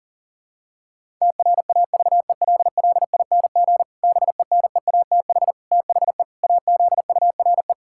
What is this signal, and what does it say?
Morse code, heard as a keyed tone, about 40 wpm, 700 Hz. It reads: TRAVELLING BENEATH THE AZURE
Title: Pangram Challenge v2 - 40wpm